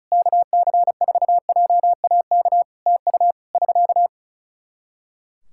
Morse code, heard as a keyed tone, about 36 wpm, 700 Hz. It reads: KC4JAK TU <SK>
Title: K1USN SST QSO Practice 35wpm